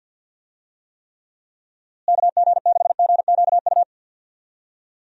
Morse code, heard as a keyed tone, about 50 wpm, 700 Hz. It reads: KG6GQU
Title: USA Call Signs - 6 Characters - Repeated 3x - 50wpm